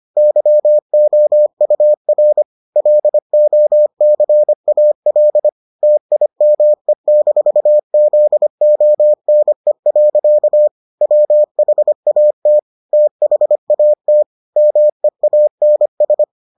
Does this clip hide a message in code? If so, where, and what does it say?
Yes: Morse code from 0.2 to 16.2 s, heard as a keyed tone, about 25 wpm, 600 Hz. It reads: YOUR LOCAL TIME-ZONE. WHAT THAT MEANS